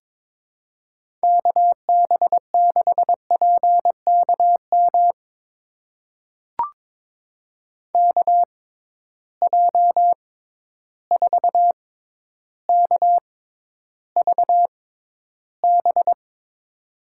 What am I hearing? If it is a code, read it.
Morse code, heard as a keyed tone, about 22 wpm, spaced as for 11 wpm, 700 Hz. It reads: KB6PKM KJ4KVB
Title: USA Call Signs - 6 Characters - 6x Character Spacing - 22wpm